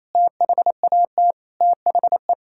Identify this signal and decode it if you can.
Morse code, heard as a keyed tone, about 28 wpm, 700 Hz. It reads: THAT THE
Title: Sentences from Top 1600 Words 28wpm